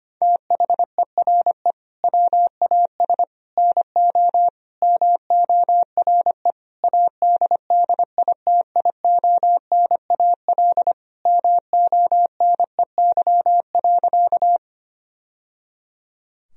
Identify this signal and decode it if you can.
Morse code, heard as a keyed tone, about 25 wpm, 700 Hz. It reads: THERE WAS NO MORE ADDITIONAL MONEY.
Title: Sentences from Top 1000 Words - Review Part 4 - 25wpm